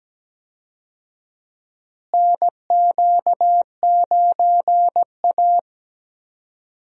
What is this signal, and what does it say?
Morse code, heard as a keyed tone, about 17 wpm, 700 Hz. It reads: NQ9A
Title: USA Call Signs - 4 Characters - Repeated 2x - 17wpm